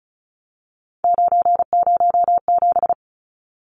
Morse code, heard as a keyed tone, about 34 wpm, 700 Hz. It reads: 907
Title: Numbers - 3-Digit 35wpm